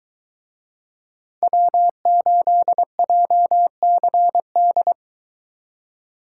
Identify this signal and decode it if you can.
Morse code, heard as a keyed tone, about 22 wpm, 700 Hz. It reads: W8JCD